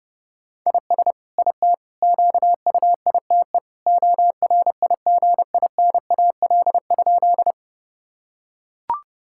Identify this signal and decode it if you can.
Morse code, heard as a keyed tone, about 32 wpm, 700 Hz. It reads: IS IT QUITE ORIGINAL?
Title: Sentences from Top 1000 Words - Review Part 6 - 30wpm